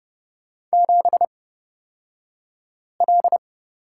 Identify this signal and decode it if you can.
Morse code, heard as a keyed tone, about 29 wpm, spaced as for 5 wpm, 690 Hz. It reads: 7L
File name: Lesson 36d - BT - Two Characters 30wpm